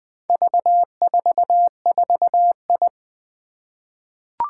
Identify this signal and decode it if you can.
Morse code, heard as a keyed tone, about 20 wpm, 700 Hz. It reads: V44I
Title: Four Challenging Letter-Number Combinations 20wpm